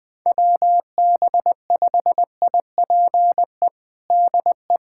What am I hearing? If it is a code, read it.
Morse code, heard as a keyed tone, about 21 wpm, 700 Hz. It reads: WB5IPE DE
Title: USA Callsign DE Callsign 20wpm